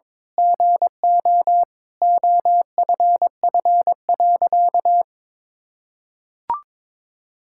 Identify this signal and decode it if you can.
Morse code, heard as a keyed tone, about 22 wpm, 700 Hz. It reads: GO OFF.
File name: practice / 22wpm / Sentences with 4-Word Phrases - 22wpm